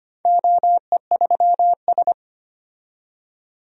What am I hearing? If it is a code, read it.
Morse code, heard as a keyed tone, about 25 wpm, 700 Hz. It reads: OE3S
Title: DXCC-100 Call Signs - 4 Characters - 25wpm